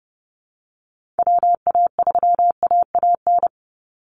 Morse code, heard as a keyed tone, about 30 wpm, 700 Hz. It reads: WA3AAN
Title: Call Signs Repeated 3x - 30wpm